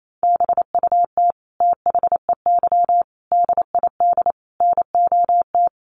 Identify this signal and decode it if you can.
Morse code, heard as a keyed tone, about 28 wpm, 700 Hz. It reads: BUT THEY DID NOT
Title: Sentences from Top 1400 Words 28wpm